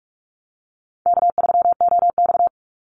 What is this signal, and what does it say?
Morse code, heard as a keyed tone, about 46 wpm, 700 Hz. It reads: K3OX
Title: Easy Call Signs - 45wpm